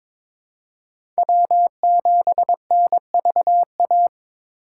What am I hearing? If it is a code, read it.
Morse code, heard as a keyed tone, about 22 wpm, 700 Hz. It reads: W7NVA
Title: USA Call Signs - 5 Characters - 3x Character Spacing - 22wpm